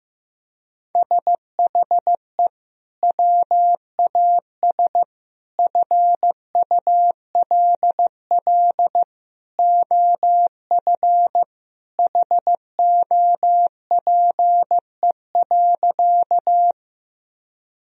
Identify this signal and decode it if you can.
Morse code, heard as a keyed tone, about 16 wpm, 700 Hz. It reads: SHE WAS FULL OF HOPE.
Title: Sentences from Top 1000 Words - Review Part 3 - 15wpm